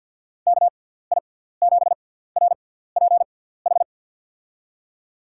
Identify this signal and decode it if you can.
Morse code, heard as a keyed tone, about 48 wpm, spaced as for 17 wpm, 700 Hz. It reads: KI7RPH
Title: USA Call Signs - 6 Characters - 6x Character Spacing - 50wpm